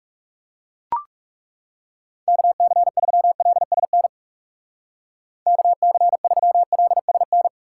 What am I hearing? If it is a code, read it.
Morse code, heard as a keyed tone, about 37 wpm, 700 Hz. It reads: KC3LSN KC3LSN
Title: Speed Racing - USA 2x3 Call Signs 30wpm